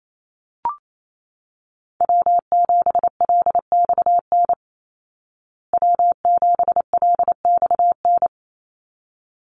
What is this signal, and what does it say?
Morse code, heard as a keyed tone, about 28 wpm, 700 Hz. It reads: W7LXN W7LXN
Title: Call Signs Repeated 3x - 28wpm